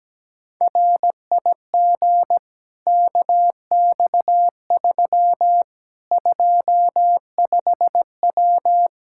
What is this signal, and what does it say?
Morse code, heard as a keyed tone, about 17 wpm, 700 Hz. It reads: RIG KX3 25W
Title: Practice QSO - Exchange 2 - Info - 17wpm